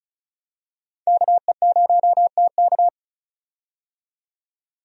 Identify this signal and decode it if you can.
Morse code, heard as a keyed tone, about 36 wpm, 700 Hz. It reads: KE0TK
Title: USA Call Signs - 5 Characters - 35wpm